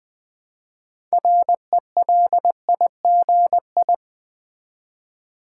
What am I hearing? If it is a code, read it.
Morse code, heard as a keyed tone, about 20 wpm, 700 Hz. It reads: RELIGI
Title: Build A Word - 3 Letters At A Time - 8-Letter Words - 20wpm